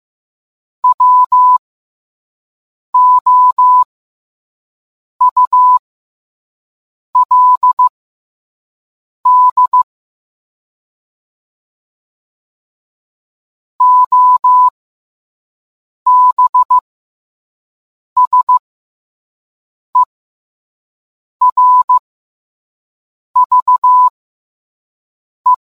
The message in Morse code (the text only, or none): WOULD OBSERVE